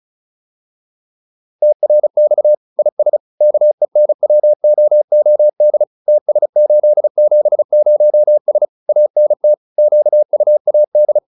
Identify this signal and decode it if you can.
Morse code, heard as a keyed tone, about 34 wpm, 600 Hz. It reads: TRX IS KENWOOD TS870S ANT QUAD